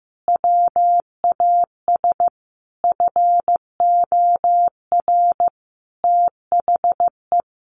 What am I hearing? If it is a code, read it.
Morse code, heard as a keyed tone, about 15 wpm, 700 Hz. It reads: WAS FOR THE
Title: Sentences from Top 1400 Words 15wpm